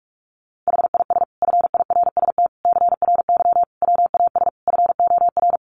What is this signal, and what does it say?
Morse code, heard as a keyed tone, about 44 wpm, 700 Hz. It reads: HIS FIRST CRY WAS FOR
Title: Sentences from Top 500 Words 45wpm